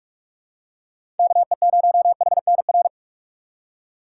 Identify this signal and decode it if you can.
Morse code, heard as a keyed tone, about 46 wpm, 700 Hz. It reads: KE0HNR
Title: USA Call Signs - 6 Characters - 6x Character Spacing - 45wpm